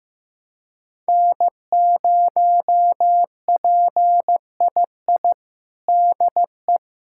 Morse code, heard as a keyed tone, about 15 wpm, 700 Hz. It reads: N0PII DE
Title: USA Callsign DE Callsign - 5 Characters - 15wpm